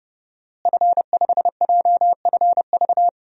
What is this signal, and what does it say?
Morse code, heard as a keyed tone, about 30 wpm, 700 Hz. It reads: F5JFV
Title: DXCC-100 Call Signs - 5 Characters - Repeated 2x - 30wpm